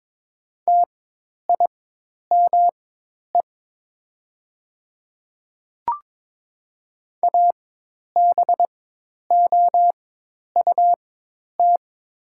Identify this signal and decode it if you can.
Morse code, heard as a keyed tone, about 22 wpm, spaced as for 10 wpm, 700 Hz. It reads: TIME ABOUT
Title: 4x Character Spacing - Top 100 Words - 22wpm